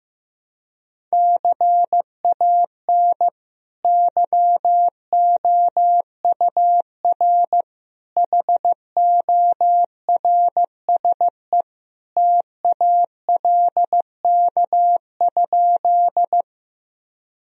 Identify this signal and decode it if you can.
Morse code, heard as a keyed tone, about 15 wpm, 700 Hz. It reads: CAN YOUR HORSE TALK?
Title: Sentences from Top 1000 Words - Review Part 1 - 15wpm